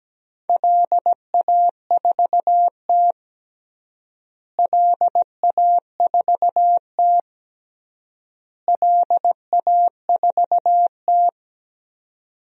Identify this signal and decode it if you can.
Morse code, heard as a keyed tone, about 17 wpm, 700 Hz. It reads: LA4T LA4T LA4T
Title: DXCC-100 Call Signs - 4 Characters - Repeated 3x - 17wpm